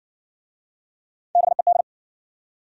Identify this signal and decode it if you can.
Morse code, heard as a keyed tone, about 59 wpm, 700 Hz. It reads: BED